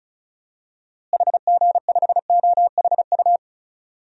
Morse code, heard as a keyed tone, about 36 wpm, 700 Hz. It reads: HG5OHU
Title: DXCC-100 Call Signs - 6 Characters - Repeated 3x - 35wpm